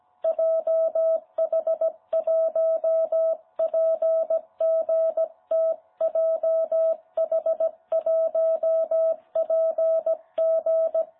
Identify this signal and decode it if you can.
Morse code, heard as a keyed tone, about 17 wpm, 660 Hz. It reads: JH1PGT JH1PG
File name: jh8cbh / 2021cw